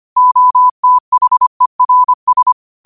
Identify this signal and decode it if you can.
Morse code, heard as a keyed tone, about 25 wpm, 1000 Hz. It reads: OTHERS